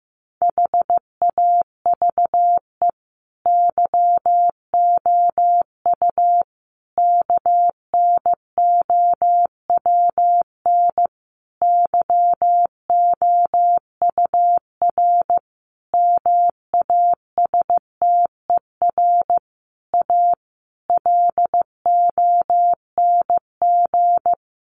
Morse code, heard as a keyed tone, about 15 wpm, 700 Hz. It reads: HAVE YOU KNOWN YOUR MASTER A LONG
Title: Sentences from Top 1300 Words 15wpm